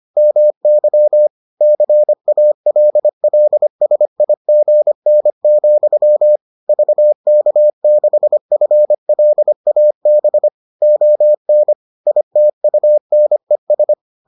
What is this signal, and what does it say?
Morse code, heard as a keyed tone, about 25 wpm, 600 Hz. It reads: MY CALLSIGN, VK6FLAB ON ITUNES